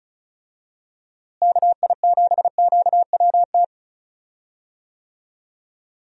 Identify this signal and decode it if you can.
Morse code, heard as a keyed tone, about 36 wpm, 700 Hz. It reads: KI7QWT
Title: USA Call Signs - 6 Characters - Repeated 2x - 35wpm